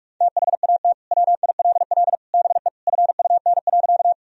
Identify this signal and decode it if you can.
Morse code, heard as a keyed tone, about 41 wpm, 700 Hz. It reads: THAT WILL BE FUN.
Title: Sentences from Top 1000 Words - Review Part 4 - 45wpm